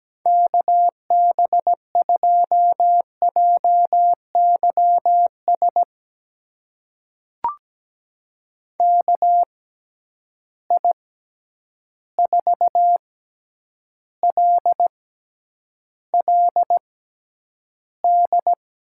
Morse code, heard as a keyed tone, about 17 wpm, spaced as for 9 wpm, 700 Hz. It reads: KB2JYS KI4LLD
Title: USA Call Signs - 6 Characters - 6x Character Spacing - 17wpm